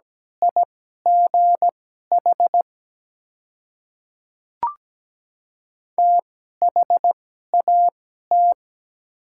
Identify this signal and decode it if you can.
Morse code, heard as a keyed tone, about 17 wpm, spaced as for 12 wpm, 700 Hz. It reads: IGH THAT
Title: 2x Character Spacing - Top 300 Words - 17wpm